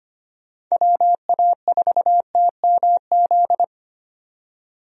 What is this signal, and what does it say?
Morse code, heard as a keyed tone, about 25 wpm, 700 Hz. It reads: WA4TMZ
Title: USA Call Signs - 6 Characters - 25wpm